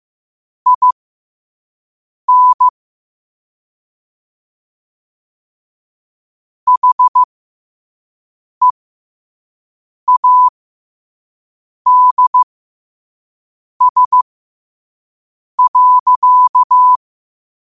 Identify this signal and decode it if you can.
Morse code, heard as a keyed tone, about 15 wpm, spaced as for 5 wpm, 1000 Hz. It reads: IN HEADS.